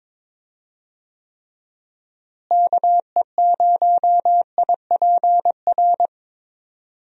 Morse code, heard as a keyed tone, about 22 wpm, 700 Hz. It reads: KE0IPR
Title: USA Call Signs - 6 Characters - Repeated 3x - 22wpm